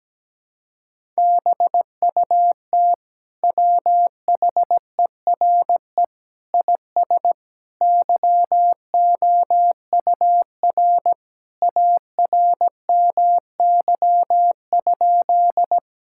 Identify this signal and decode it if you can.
Morse code, heard as a keyed tone, about 17 wpm, 700 Hz. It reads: BUT WHERE IS YOUR ARMY?
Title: Sentences from Top 1000 Words - Review Part 5 - 17wpm